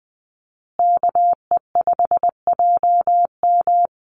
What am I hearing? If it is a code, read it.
Morse code, heard as a keyed tone, about 20 wpm, 700 Hz. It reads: KE5JM
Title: USA Call Signs Repeated 2x 20wpm